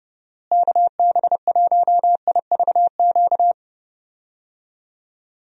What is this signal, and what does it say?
Morse code, heard as a keyed tone, about 30 wpm, 700 Hz. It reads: KB1IVQ
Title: USA Call Signs - 6 Characters - Repeated 3x - 30wpm